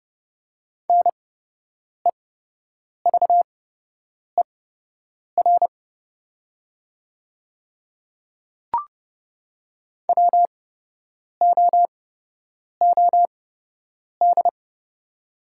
Morse code, heard as a keyed tone, about 30 wpm, spaced as for 8 wpm, 700 Hz. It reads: NEVER WOOD
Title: 8x Character Spacing - Top 300 Words - 30wpm